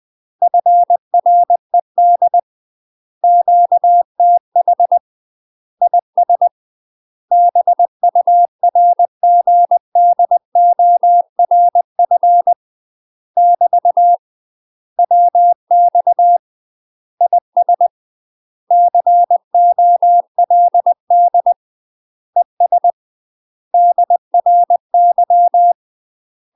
Morse code, heard as a keyed tone, about 20 wpm, 700 Hz. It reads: FRED QTH IS BURGDORF <BT> WX IS COLD ES DRY
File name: UK MASTER QSOs TRAINING / 20WPM / UK MASTER QSOs 0010